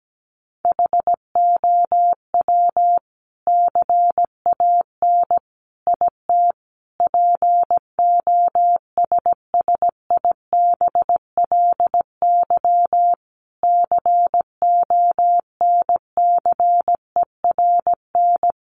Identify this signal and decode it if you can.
Morse code, heard as a keyed tone, about 17 wpm, 700 Hz. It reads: HOW CAN IT POSSIBLY CONCERN
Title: Sentences from Top 1300 Words 17wpm